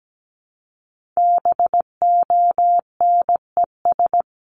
Morse code, heard as a keyed tone, about 17 wpm, 700 Hz. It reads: BONES
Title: A Sound Of Thunder 17wpm